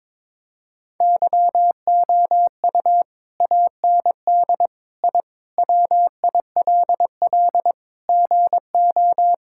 Morse code, heard as a keyed tone, about 22 wpm, 700 Hz. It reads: YOU AND I WILL GO
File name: practice / 22wpm / Sentences with 2-Word Phrases - 22wpm